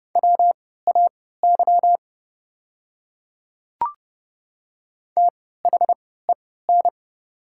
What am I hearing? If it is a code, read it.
Morse code, heard as a keyed tone, about 32 wpm, spaced as for 17 wpm, 700 Hz. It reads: WAY THEN
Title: 3x Character Spacing - Top 500 Words - 30wpm